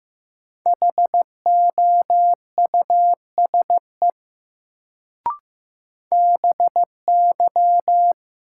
Morse code, heard as a keyed tone, about 15 wpm, 700 Hz. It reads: HOUSE BY
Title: Sets of 2 Words - New Word Drill - 15wpm